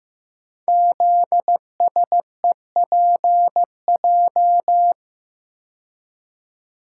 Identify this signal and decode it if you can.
Morse code, heard as a keyed tone, about 15 wpm, 700 Hz. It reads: ZSEPJ
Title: Code Groups - 5 Letters - 15wpm